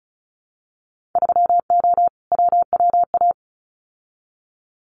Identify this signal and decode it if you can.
Morse code, heard as a keyed tone, about 36 wpm, 700 Hz. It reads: 3O WWA
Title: Winter Field Day Exchanges Encore 35wpm